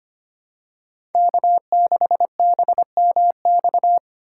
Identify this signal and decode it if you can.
Morse code, heard as a keyed tone, about 25 wpm, 700 Hz. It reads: K6BMX
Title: USA Call Signs - 2x Character Spacing - 25wpm